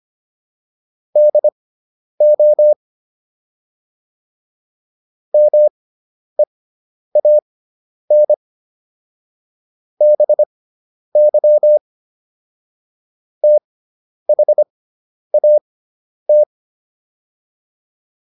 Morse code, heard as a keyed tone, about 25 wpm, spaced as for 10 wpm, 600 Hz. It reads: DO MEAN BY THAT